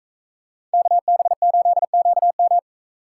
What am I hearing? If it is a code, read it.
Morse code, heard as a keyed tone, about 43 wpm, 700 Hz. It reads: KB8QM